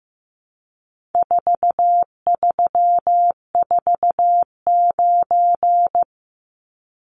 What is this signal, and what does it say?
Morse code, heard as a keyed tone, about 15 wpm, 700 Hz. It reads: 4349
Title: Numbers - 4-Digit 15wpm